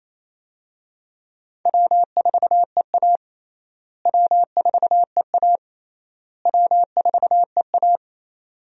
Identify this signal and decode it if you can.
Morse code, heard as a keyed tone, about 28 wpm, 700 Hz. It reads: W4EA W4EA W4EA
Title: USA Call Signs - 4 Characters - Repeated 3x - 28wpm